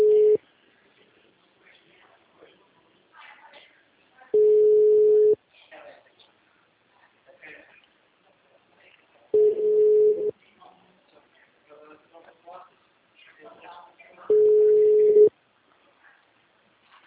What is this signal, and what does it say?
Ringback tone (Europe, 425 Hz), with speech.